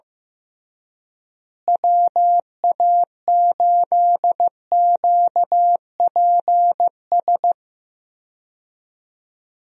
Morse code, heard as a keyed tone, about 15 wpm, 700 Hz. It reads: WA8QPS
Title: USA Call Signs - 6 Characters - Repeated 3x - 15wpm